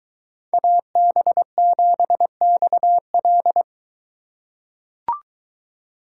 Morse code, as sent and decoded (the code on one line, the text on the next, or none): .- -... --... -..- .-..
AB7XL